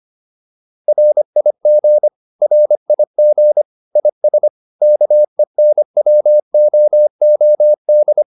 Morse code, heard as a keyed tone, about 25 wpm, 600 Hz. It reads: RIG RIG IS KENWOOD